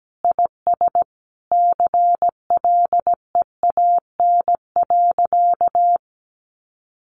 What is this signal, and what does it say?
Morse code, heard as a keyed tone, about 17 wpm, 700 Hz. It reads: IS CLEAN.
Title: Sentences from Top 1100 Words 17wpm